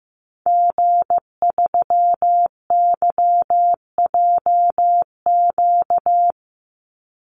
Morse code, heard as a keyed tone, about 15 wpm, 700 Hz. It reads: G3YJQ
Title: Call Signs Repeated 2x - 15wpm